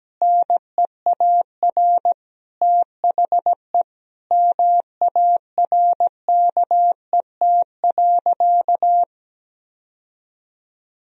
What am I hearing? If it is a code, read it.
Morse code, heard as a keyed tone, about 17 wpm, 700 Hz. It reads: NEAR THE MARKET.
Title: Sentences from Top 1000 Words - Review Part 3 - 17wpm